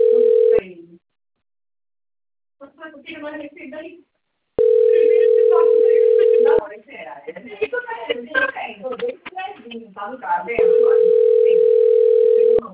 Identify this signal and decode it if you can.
Ringback tone (North America), 440 + 480 Hz, with speech.
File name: rg-1600-unknown- - - .69742